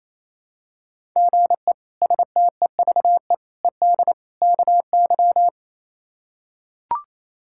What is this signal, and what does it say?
Morse code, heard as a keyed tone, about 27 wpm, 700 Hz. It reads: GE STEVE ED KY
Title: K1USN SST QSO Practice - 28wpm